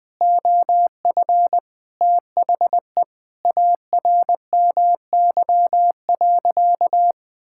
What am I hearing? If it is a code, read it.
Morse code, heard as a keyed tone, about 20 wpm, 700 Hz. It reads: OF THE ARMY.
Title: Sentences from Top 1000 Words - Review Part 6 - 20wpm